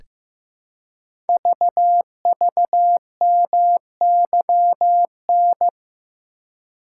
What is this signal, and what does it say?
Morse code, heard as a keyed tone, about 15 wpm, 700 Hz. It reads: VVMYN
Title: Code Groups - 5 Letters - 15wpm